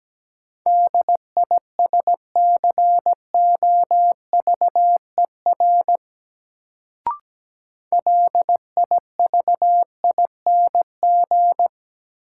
Morse code, heard as a keyed tone, about 17 wpm, 700 Hz. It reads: DISCOVER LIVING
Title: Sentences from Top 1100 Words - New Word Drill Alt - 17wpm